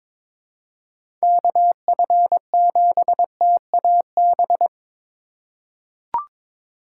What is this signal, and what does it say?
Morse code, heard as a keyed tone, about 22 wpm, 700 Hz. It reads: KF7TAB
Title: USA Call Signs - 6 Characters - 3x Character Spacing - 22wpm